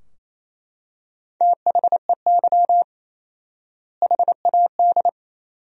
Morse code, heard as a keyed tone, about 28 wpm, 700 Hz. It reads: THEY HAD
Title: 4x Word-Spacing - Sentences from Top 400 Words - 28wpm